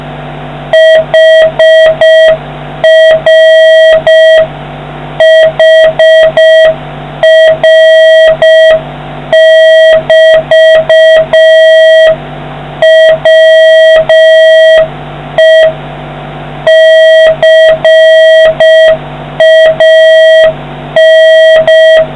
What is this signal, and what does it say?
Morse code, heard as a keyed tone, about 6 wpm, 1950 Hz, with noise. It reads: HR HR<BT>WE CAN